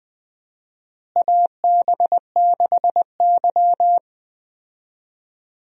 Morse code, heard as a keyed tone, about 20 wpm, 700 Hz. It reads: AB6Y
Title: USA Call Signs - 4 Characters - Repeated 2x - 20wpm